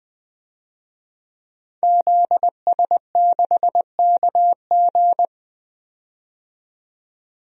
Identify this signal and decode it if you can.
Morse code, heard as a keyed tone, about 21 wpm, 700 Hz. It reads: ZS6KG
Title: DXCC-100 Call Signs - 5 Characters - 20wpm